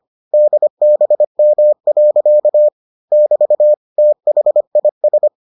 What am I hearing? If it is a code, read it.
Morse code, heard as a keyed tone, about 25 wpm, 600 Hz. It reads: DBM. <BT> THIS